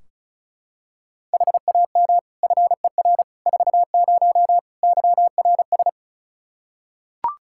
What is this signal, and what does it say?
Morse code, heard as a keyed tone, about 36 wpm, 700 Hz. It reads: HAM FER 40 YRS
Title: Practice QSO - Exchange 3 - More Info - 35wpm